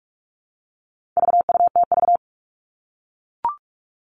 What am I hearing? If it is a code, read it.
Morse code, heard as a keyed tone, about 44 wpm, 700 Hz. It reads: VUTV
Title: All Cut Numbers - 4-Digit 45wpm